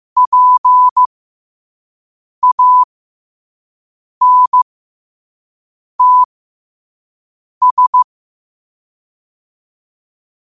Morse code, heard as a keyed tone, about 15 wpm, spaced as for 5 wpm, 1000 Hz. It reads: PANTS
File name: qotdfiles / 5wpm